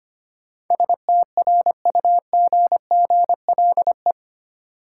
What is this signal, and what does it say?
Morse code, heard as a keyed tone, about 25 wpm, 700 Hz. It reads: STRUGGLE
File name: Eight Letter Words - 25wpm